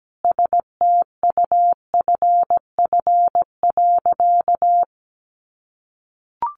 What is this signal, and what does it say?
Morse code, heard as a keyed tone, about 17 wpm, 700 Hz. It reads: STUFF.
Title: Sentences from Top 1400 Words 17wpm